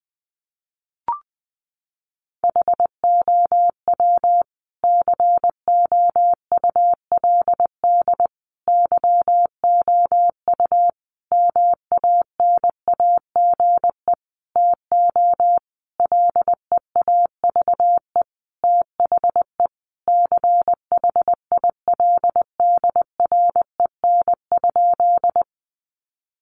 Morse code, heard as a keyed tone, about 20 wpm, 700 Hz. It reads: HOW COULD YOU MANAGE TO LEAVE THE CHILDREN?
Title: Sentences from Top 1400 Words 20wpm